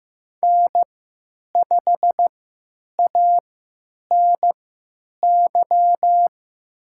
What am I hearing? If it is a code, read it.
Morse code, heard as a keyed tone, about 15 wpm, spaced as for 9 wpm, 700 Hz. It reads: N5ANY
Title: USA Call Signs - 5 Characters - 3x Character Spacing - 15wpm